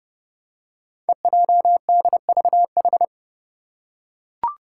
Morse code, heard as a keyed tone, about 30 wpm, 700 Hz. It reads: EJDVH
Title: Five Challenging Letter-Number Combinations 30wpm